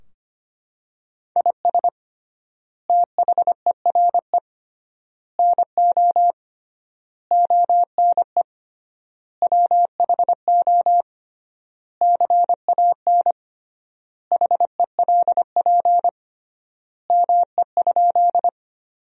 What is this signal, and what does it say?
Morse code, heard as a keyed tone, about 25 wpm, 700 Hz. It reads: IS THERE NO ONE WHO CAN HELP ME?
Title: 3x Word-Spacing - Sentences from Top 200 Words - 25wpm